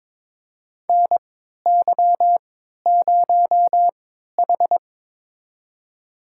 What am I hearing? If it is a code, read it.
Morse code, heard as a keyed tone, about 22 wpm, spaced as for 13 wpm, 700 Hz. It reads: NY0H
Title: USA Call Signs - 4 Characters - 3x Character Spacing - 22wpm